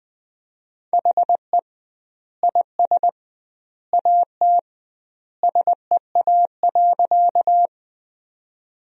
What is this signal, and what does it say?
Morse code, heard as a keyed tone, about 21 wpm, 700 Hz. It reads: HE IS AT SEA.
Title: 2x Word-Spacing - Sentences from Top 300 Words - 20wpm